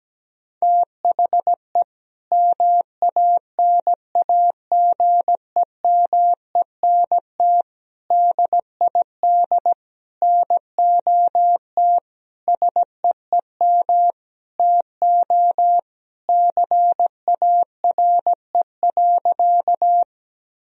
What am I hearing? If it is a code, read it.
Morse code, heard as a keyed tone, about 17 wpm, 700 Hz. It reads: THE MANAGEMENT DID NOT SEEM TO CARE.
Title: Sentences from Top 1000 Words - Review Part 4 - 17wpm